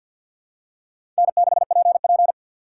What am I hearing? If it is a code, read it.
Morse code, heard as a keyed tone, about 51 wpm, 700 Hz. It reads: N6PP
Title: USA Call Signs - 4 Characters - 2x Character Spacing - 50wpm